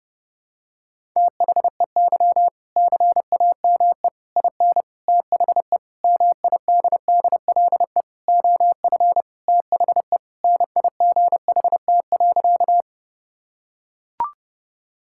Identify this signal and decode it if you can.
Morse code, heard as a keyed tone, about 32 wpm, 700 Hz. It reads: THEY CAME IN THE MIDDLE OF THE NIGHT.
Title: Sentences from Top 1000 Words - Review Part 6 - 30wpm